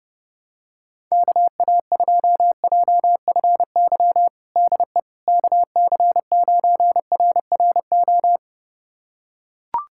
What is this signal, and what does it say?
Morse code, heard as a keyed tone, about 32 wpm, 690 Hz. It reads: KA2JFY DE KC9RRO
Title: USA Callsign DE Callsign - 6 Characters - 30wpm